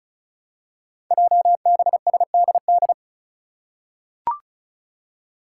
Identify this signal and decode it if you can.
Morse code, heard as a keyed tone, about 36 wpm, 700 Hz. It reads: JBSDD
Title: Five Challenging Letter-Number Combinations 35wpm